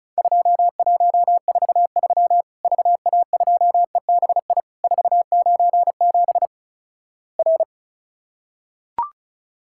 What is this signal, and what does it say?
Morse code, heard as a keyed tone, about 36 wpm, 700 Hz. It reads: 2143 VA2EBI 497 R
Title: Worked All Europe DX Contest - QTC Exchange - 35wpm